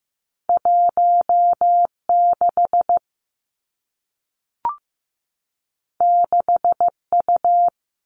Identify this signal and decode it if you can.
Morse code, heard as a keyed tone, about 15 wpm, 700 Hz. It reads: 16 6U
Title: Two Challenging Letter-Number Combinations 15wpm